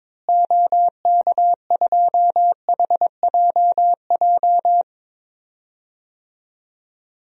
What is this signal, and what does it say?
Morse code, heard as a keyed tone, about 21 wpm, 700 Hz. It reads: OK2HJJ
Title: DXCC-100 Call Signs - 6 Characters - Repeated 2x - 22wpm